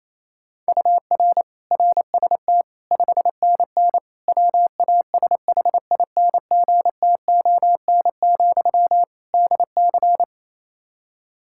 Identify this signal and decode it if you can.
Morse code, heard as a keyed tone, about 28 wpm, 700 Hz. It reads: UR RST 5NN WASHINGTON, DC
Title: Practice QSO - Exchange 1 - Beginning - 28wpm